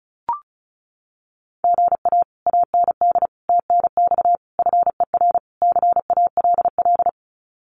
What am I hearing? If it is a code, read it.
Morse code, heard as a keyed tone, about 36 wpm, 700 Hz. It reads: GA AND TNX FER CALL
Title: Common QSO Phrases Encore 35wpm